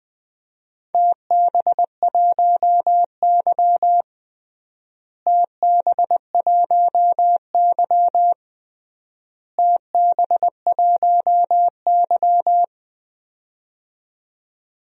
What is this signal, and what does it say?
Morse code, heard as a keyed tone, about 21 wpm, 700 Hz. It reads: TB1Y TB1Y TB1Y
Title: DXCC-100 Call Signs - 4 Characters - Repeated 3x - 20wpm